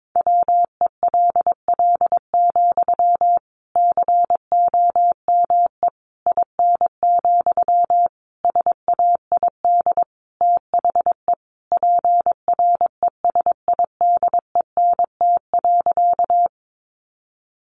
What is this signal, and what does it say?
Morse code, heard as a keyed tone, about 22 wpm, 700 Hz. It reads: WELL, COME IN, SAID THE PRESIDENT.
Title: Sentences from Top 1000 Words 22wpm